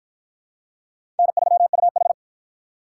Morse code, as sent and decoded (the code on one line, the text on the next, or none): -. ...-- ..- ....
N3UH